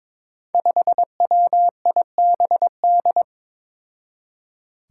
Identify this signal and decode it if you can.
Morse code, heard as a keyed tone, about 22 wpm, 700 Hz. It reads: 5WIBD
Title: Five Challenging Letter-Number Combinations 22wpm